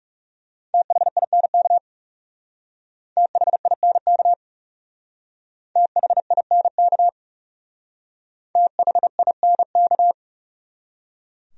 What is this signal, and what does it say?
Morse code, heard as a keyed tone, about 36 wpm, 700 Hz. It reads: THINK THINK THINK THINK
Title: Speed-Racing Top 500 Words 30wpm